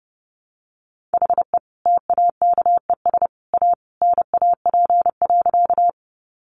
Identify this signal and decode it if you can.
Morse code, heard as a keyed tone, about 30 wpm, 700 Hz. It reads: HE TAKES A NAP.
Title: Easy Phrases 30wpm